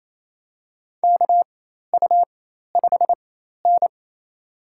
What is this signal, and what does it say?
Morse code, heard as a keyed tone, about 27 wpm, spaced as for 13 wpm, 700 Hz. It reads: KU5N
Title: USA Call Signs - 4 Characters - 4x Character Spacing - 28wpm